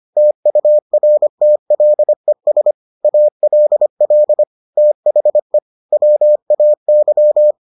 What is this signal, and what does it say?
Morse code, heard as a keyed tone, about 25 wpm, 600 Hz. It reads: TURTLES ALL THE WAY